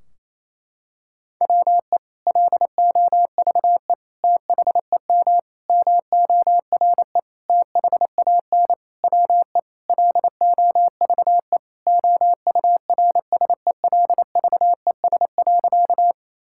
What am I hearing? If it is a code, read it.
Morse code, heard as a keyed tone, about 28 wpm, 700 Hz. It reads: WE LOVE THEM MORE THAN WE LOVE OURSELVES.
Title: Sentences from Top 1600 Words 28wpm